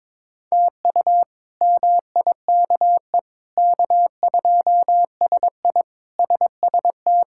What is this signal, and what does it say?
Morse code, heard as a keyed tone, about 22 wpm, 700 Hz. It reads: TU MIKE K2SI SST
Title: K1USN SST QSO Practice - 22wpm